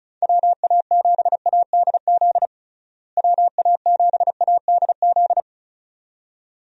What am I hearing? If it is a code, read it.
Morse code, heard as a keyed tone, about 36 wpm, 700 Hz. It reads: WA7ADZ WA7ADZ
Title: USA Call Signs - 6 Characters - Repeated 2x - 35wpm